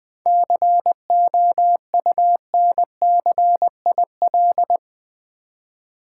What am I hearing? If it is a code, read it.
Morse code, heard as a keyed tone, about 20 wpm, 690 Hz. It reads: COUNCIL
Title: 7 Letter Words - First 3 Letters Given - 20wpm